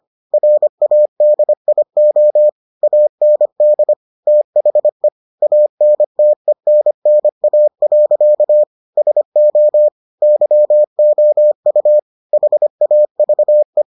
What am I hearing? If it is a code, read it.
Morse code, heard as a keyed tone, about 25 wpm, 600 Hz. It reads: RADIO AND THE ANTENNA. SO YOU HAVE